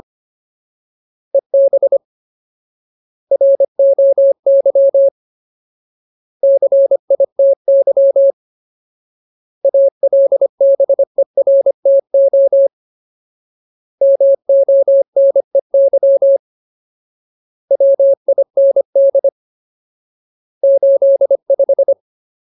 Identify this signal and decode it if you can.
Morse code, heard as a keyed tone, about 25 wpm, 550 Hz. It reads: EB ROY CITY ALBERTO MONEY WIND 85
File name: Ultimate Word List / 25WPM Ultimate